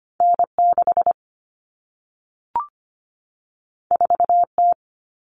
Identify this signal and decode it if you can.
Morse code, heard as a keyed tone, about 25 wpm, 700 Hz. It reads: N6 4T
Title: All Cut Numbers - 2-Digit 25wpm